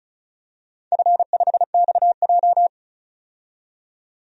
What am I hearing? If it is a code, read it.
Morse code, heard as a keyed tone, about 36 wpm, 700 Hz. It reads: F5XJ
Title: DXCC-100 Call Signs - 4 Characters - 35wpm